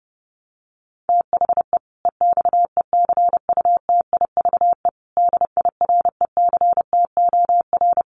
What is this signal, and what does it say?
Morse code, heard as a keyed tone, about 30 wpm, 700 Hz. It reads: THE EXECUTIVE DIRECTOR
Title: Sets of 5 Words Encore 30wpm